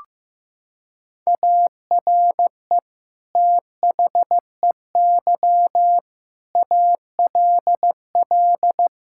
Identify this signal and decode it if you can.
Morse code, heard as a keyed tone, about 15 wpm, 700 Hz. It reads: ARE THEY ALL
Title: Sentences from Top 1600 Words 15wpm